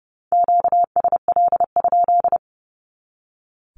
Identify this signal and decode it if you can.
Morse code, heard as a keyed tone, about 30 wpm, 700 Hz. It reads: QSL?
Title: Q-Codes 30wpm